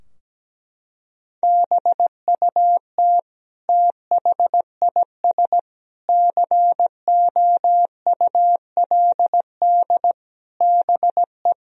Morse code, heard as a keyed tone, about 17 wpm, 700 Hz. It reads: BUT THIS COULD BE
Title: Sentences from Top 1000 Words - Review Part 2 - 17wpm